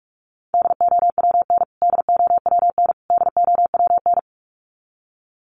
Morse code, heard as a keyed tone, about 44 wpm, 700 Hz. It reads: DOWN DOWN DOWN
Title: Top 200 Words Repeated 45wpm